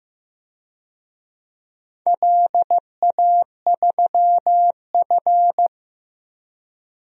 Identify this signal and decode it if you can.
Morse code, heard as a keyed tone, about 15 wpm, 700 Hz. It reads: LA3F
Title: DXCC-100 Call Signs - 4 Characters - 15wpm